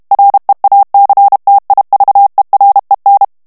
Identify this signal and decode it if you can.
Morse code, heard as a keyed tone, about 32 wpm, 800 Hz. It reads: REACTIVEREN
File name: HOLLANDAIS texte clair 2 vitesse 30 mots minute